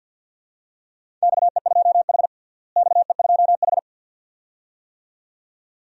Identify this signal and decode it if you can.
Morse code, heard as a keyed tone, about 47 wpm, 700 Hz. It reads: XE2H XE2H
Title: DXCC-100 Call Signs - 4 Characters - Repeated 2x - 50wpm